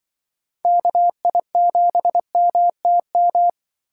Morse code, heard as a keyed tone, about 24 wpm, 700 Hz. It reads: KI7MTM
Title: Speed Racing - USA 2x3 Call Signs 20wpm